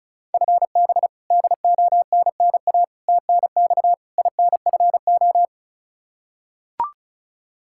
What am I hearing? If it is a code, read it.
Morse code, heard as a keyed tone, about 34 wpm, 700 Hz. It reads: FB DONNA TNX INFO
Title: Practice QSO - Exchange 3 - More Info - 35wpm